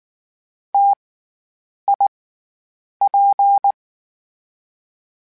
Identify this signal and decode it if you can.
Morse code, heard as a keyed tone, about 19 wpm, 800 Hz. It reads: T I P
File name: Variable Spacing / Variable Spacing-mp3s / T AOEs @18